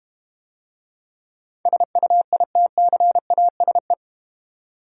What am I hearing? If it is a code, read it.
Morse code, heard as a keyed tone, about 31 wpm, 700 Hz. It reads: SUITCASE
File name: Easy Words - Speed-Racing with Hints - 25wpm